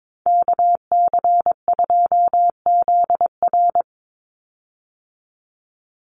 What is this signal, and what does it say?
Morse code, heard as a keyed tone, about 22 wpm, 700 Hz. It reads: KC2ZR
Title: USA Call Signs 22wpm